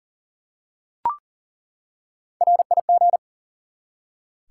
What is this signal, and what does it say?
Morse code, heard as a keyed tone, about 42 wpm, 700 Hz. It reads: RIG
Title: Top 180 QSO Elements 40wpm